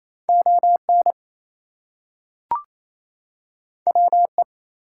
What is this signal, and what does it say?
Morse code, heard as a keyed tone, about 29 wpm, 700 Hz. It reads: ON WE
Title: Two Letter Words - 28wpm